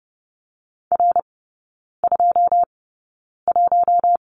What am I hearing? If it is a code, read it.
Morse code, heard as a keyed tone, about 30 wpm, spaced as for 9 wpm, 700 Hz. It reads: R21
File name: Lesson 18e - Letter m - Three Characters 30wpm